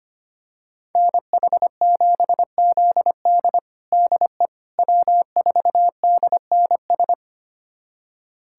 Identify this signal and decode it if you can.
Morse code, heard as a keyed tone, about 25 wpm, 700 Hz. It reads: NH7ZD DE W4DNS
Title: USA Callsign DE Callsign - 5 Characters - 25wpm